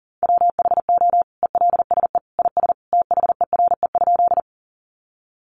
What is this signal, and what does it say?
Morse code, heard as a keyed tone, about 37 wpm, 700 Hz. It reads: WHO ELSE IS THERE?
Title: Sentences from Top 800 Words 40wpm